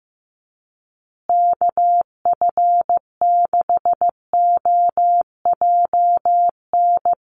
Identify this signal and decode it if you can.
Morse code, heard as a keyed tone, about 15 wpm, 700 Hz. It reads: KF6OJN
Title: USA Call Signs Repeated 2x 15wpm